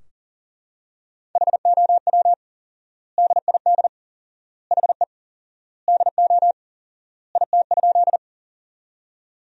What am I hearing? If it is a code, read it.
Morse code, heard as a keyed tone, about 40 wpm, 700 Hz. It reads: HOW DID HE DO IT?
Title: 4x Word-Spacing - Sentences from Top 100 Words - 40wpm